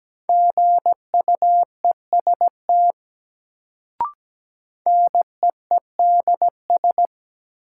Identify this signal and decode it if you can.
Morse code, heard as a keyed tone, about 17 wpm, 700 Hz. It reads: GUEST NEEDS
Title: Sentences from Top 1600 Words - New Word Drill Alt - 17wpm